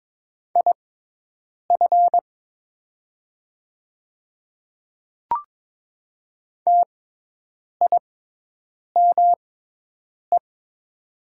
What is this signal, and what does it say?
Morse code, heard as a keyed tone, about 22 wpm, spaced as for 8 wpm, 700 Hz. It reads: IF TIME
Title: 6x Character Spacing - Top 100 Words - 22wpm